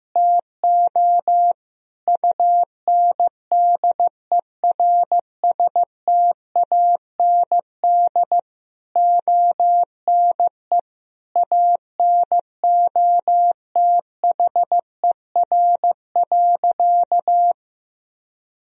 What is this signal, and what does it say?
Morse code, heard as a keyed tone, about 15 wpm, 700 Hz. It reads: TO UNDERSTAND ONE ANOTHER.
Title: Sentences from Top 1000 Words - Review Part 1 - 15wpm